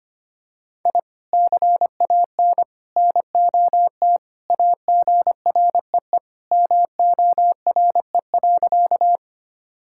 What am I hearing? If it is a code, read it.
Morse code, heard as a keyed tone, about 25 wpm, 700 Hz. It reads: I CAN NOT AGREE MORE.
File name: practice / 25wpm / Sentences with 3-Word Phrases - 25wpm